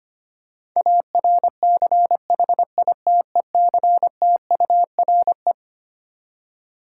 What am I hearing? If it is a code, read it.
Morse code, heard as a keyed tone, about 25 wpm, 700 Hz. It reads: ARCHITECTURE
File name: Twelve Letter Words - 25wpm